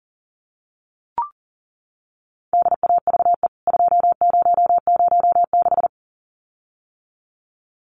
Morse code, heard as a keyed tone, about 40 wpm, 700 Hz. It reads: DAVE 2006
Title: CWT Exchange 40wpm